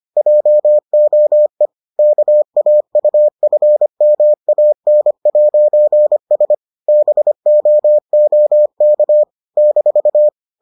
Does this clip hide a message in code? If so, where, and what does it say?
Yes: Morse code from 0.2 to 10.3 s, heard as a keyed tone, about 25 wpm, 600 Hz. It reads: JOE KAUFMAN'S BOOK -